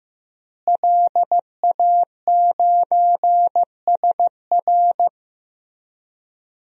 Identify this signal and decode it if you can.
Morse code, heard as a keyed tone, about 15 wpm, 700 Hz. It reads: LA9SR
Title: DXCC-100 Call Signs - 5 Characters - Repeated 3x - 15wpm